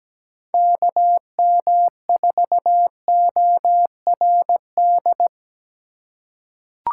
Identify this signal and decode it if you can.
Morse code, heard as a keyed tone, about 17 wpm, 700 Hz. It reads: KM4ORD
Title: USA Call Signs - 6 Characters - 4x Character Spacing - 17wpm